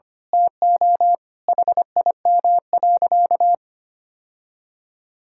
Morse code, heard as a keyed tone, about 25 wpm, 700 Hz. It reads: TO HIM.
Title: Sentences from Top 1000 Words - Review Part 6 - 25wpm